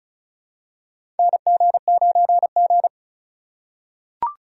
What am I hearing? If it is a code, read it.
Morse code, heard as a keyed tone, about 36 wpm, 700 Hz. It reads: NG9G
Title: USA Call Signs - 4 Characters - 2x Character Spacing - 35wpm